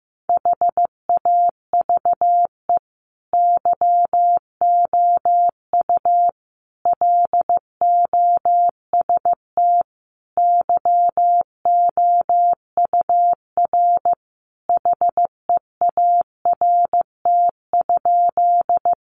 Morse code, heard as a keyed tone, about 15 wpm, 700 Hz. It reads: HAVE YOU LOST YOUR HEART?
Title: Sentences from Top 1400 Words 15wpm